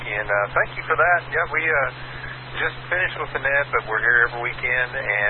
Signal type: Emergency Alert System (SAME) tones